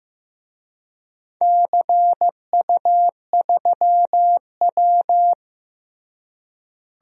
Morse code, heard as a keyed tone, about 16 wpm, 700 Hz. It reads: CU3W
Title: DXCC-100 Call Signs - 4 Characters - Repeated 3x - 15wpm